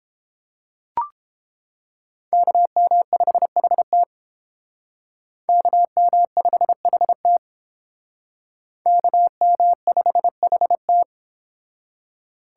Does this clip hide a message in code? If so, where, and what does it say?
Yes: Morse code from 2.3 to 11.0 s, heard as a keyed tone, about 30 wpm, 700 Hz. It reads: KM5HT KM5HT KM5HT